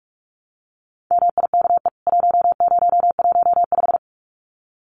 Morse code, heard as a keyed tone, about 44 wpm, 700 Hz. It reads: MIKE 1015
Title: CWT Exchange 45wpm